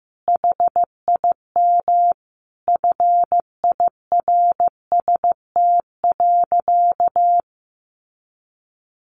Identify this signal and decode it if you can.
Morse code, heard as a keyed tone, about 15 wpm, 700 Hz. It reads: HIM FIRST.
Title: Sentences from Top 1100 Words 15wpm